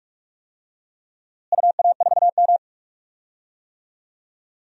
Morse code, heard as a keyed tone, about 45 wpm, 700 Hz. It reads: UA4M
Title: DXCC-100 Call Signs - 4 Characters - Repeated 2x - 45wpm